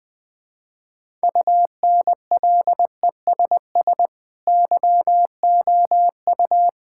Morse code, heard as a keyed tone, about 20 wpm, 700 Hz. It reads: UNLESS YOU
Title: Sentences from Top 1000 Words - Review Part 6 - 20wpm